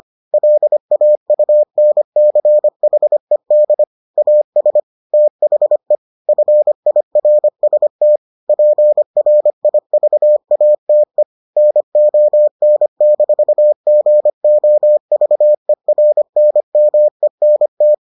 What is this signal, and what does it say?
Morse code, heard as a keyed tone, about 25 wpm, 600 Hz. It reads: LAUNCHED AS THE FIRST PRIVATE NON-GOVERNMENT